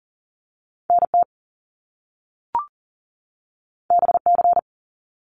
Morse code, heard as a keyed tone, about 40 wpm, 700 Hz. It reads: NT BC
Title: Canadian Provinces and Territories 40wpm